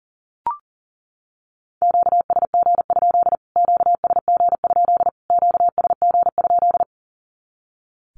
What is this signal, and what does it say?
Morse code, heard as a keyed tone, about 40 wpm, 700 Hz. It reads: QSG? QSG? QSG?
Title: Q-Codes Repeated 3x 40wpm